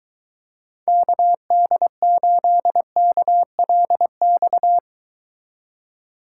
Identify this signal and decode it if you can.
Morse code, heard as a keyed tone, about 23 wpm, 700 Hz. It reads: KD8KLX